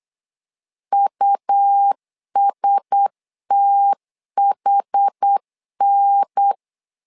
Morse code, heard as a keyed tone, about 8 wpm, 800 Hz. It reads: USTHN
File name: LEZIONI / 5 LBVC / 5 L 8wpm 2831